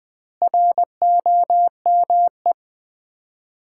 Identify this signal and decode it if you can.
Morse code, heard as a keyed tone, about 20 wpm, 700 Hz. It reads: ROME
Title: Common QSO Phrases - 2x Word-Spacing - 20wpm